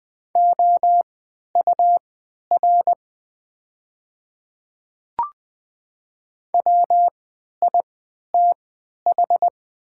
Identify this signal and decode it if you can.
Morse code, heard as a keyed tone, about 20 wpm, spaced as for 11 wpm, 700 Hz. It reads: OUR WITH
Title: 3x Character Spacing - Top 300 Words - 20wpm